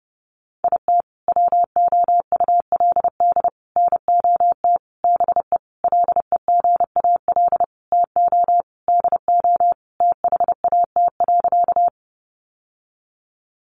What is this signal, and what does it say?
Morse code, heard as a keyed tone, about 30 wpm, 700 Hz. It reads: IT WOULD NOT BE LEGAL TO DO THAT.
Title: Sentences from Top 1200 Words 30wpm